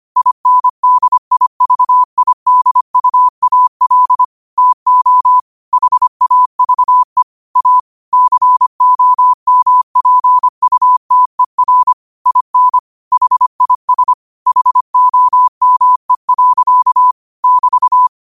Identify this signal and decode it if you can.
Morse code, heard as a keyed tone, about 25 wpm, 1000 Hz. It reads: INDIVIDUAL TO HAVE A COMPUTER IN HIS HOME. <BT>
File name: qotdfiles / 25wpm